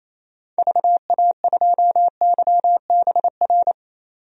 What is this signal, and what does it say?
Morse code, heard as a keyed tone, about 28 wpm, 700 Hz. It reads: VA2YBR
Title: DXCC-100 Call Signs - 6 Characters - 28wpm